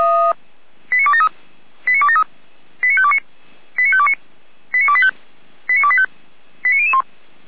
Five-tone selective call (ZVEI), heard as quick runs of five tones, tones of about 70 ms. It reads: 89273 89273 89529 89529 89285 89285 89001